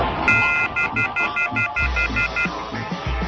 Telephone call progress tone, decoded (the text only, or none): off-hook warning tone